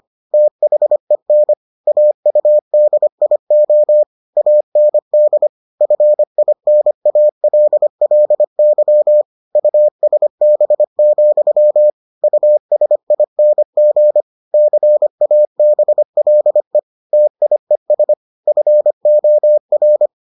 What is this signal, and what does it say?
Morse code, heard as a keyed tone, about 25 wpm, 600 Hz. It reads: THEN AUDIO AND FINALLY USB, USING CABLE TIES FOR